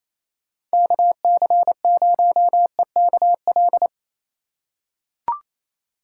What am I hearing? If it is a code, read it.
Morse code, heard as a keyed tone, about 28 wpm, 700 Hz. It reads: KC0EKL
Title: USA Call Signs - 6 Characters - 6x Character Spacing - 28wpm